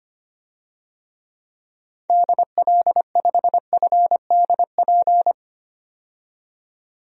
Morse code, heard as a keyed tone, about 25 wpm, 700 Hz. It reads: DL5FDP
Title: DXCC-100 Call Signs - 6 Characters - 25wpm